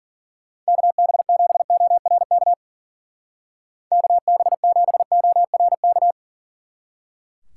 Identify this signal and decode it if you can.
Morse code, heard as a keyed tone, about 49 wpm, 700 Hz. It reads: KB7ORK KB7ORK
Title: Speed Racing - USA 2x3 Call Signs 40wpm